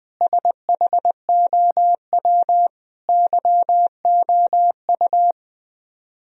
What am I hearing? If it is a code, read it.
Morse code, heard as a keyed tone, about 20 wpm, 700 Hz. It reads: SHOW YOU